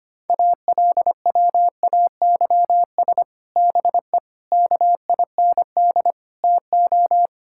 Morse code, heard as a keyed tone, about 25 wpm, 700 Hz. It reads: ALWAYS BE KIND TO
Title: Sentences from Top 1600 Words 25wpm